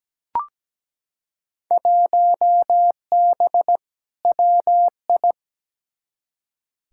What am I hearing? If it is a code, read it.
Morse code, heard as a keyed tone, about 17 wpm, 700 Hz. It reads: 1B WI
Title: Field Day Exchanges - 17wpm